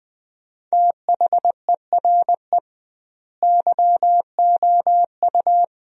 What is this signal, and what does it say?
Morse code, heard as a keyed tone, about 20 wpm, 700 Hz. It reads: THERE YOU
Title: 2x Word-Spacing - Sentences from Top 100 Words - 20wpm